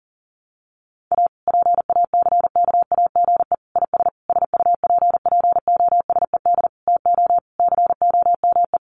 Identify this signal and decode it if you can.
Morse code, heard as a keyed tone, about 40 wpm, 700 Hz. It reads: A PACKAGE IS SUPPOSED TO COME
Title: Additional Easy Phrases 40wpm